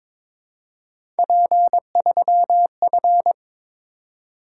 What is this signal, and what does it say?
Morse code, heard as a keyed tone, about 23 wpm, 700 Hz. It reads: P3F
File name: practice / 22wpm / Three Letter-Number Combinations - 22wpm